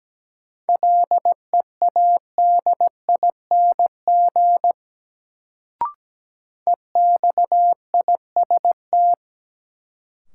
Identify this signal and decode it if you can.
Morse code, heard as a keyed tone, about 17 wpm, 700 Hz. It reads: LEADING EXIST
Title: Sentences from Top 1100 Words - New Word Drill Alt - 17wpm